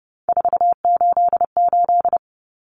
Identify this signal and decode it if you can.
Morse code, heard as a keyed tone, about 30 wpm, 700 Hz. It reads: 488
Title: Numbers - 3-Digit 30wpm